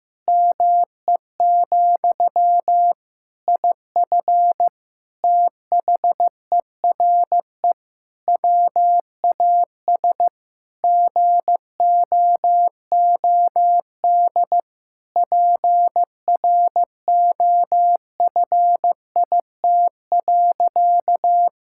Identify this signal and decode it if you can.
Morse code, heard as a keyed tone, about 15 wpm, 700 Hz. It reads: ME, IF THERE WAS GOOD PROFIT.
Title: Sentences from Top 1600 Words 15wpm